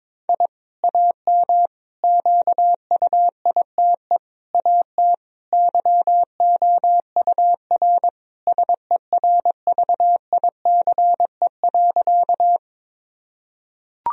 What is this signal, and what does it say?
Morse code, heard as a keyed tone, about 22 wpm, 700 Hz. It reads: I AM QUITE AT YOUR SERVICE.
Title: Sentences from Top 1000 Words - Review Part 4 - 22wpm